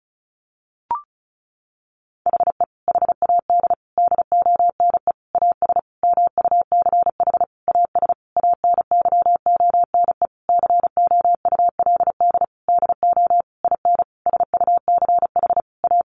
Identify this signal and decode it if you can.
Morse code, heard as a keyed tone, about 34 wpm, 700 Hz. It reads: HE HAD DONE AS MUCH AS ANYONE COULD DO IN SUCH A
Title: Sentences from Top 1200 Words 35wpm